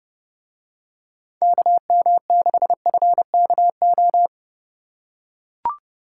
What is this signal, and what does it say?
Morse code, heard as a keyed tone, about 30 wpm, 700 Hz. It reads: KM6FKO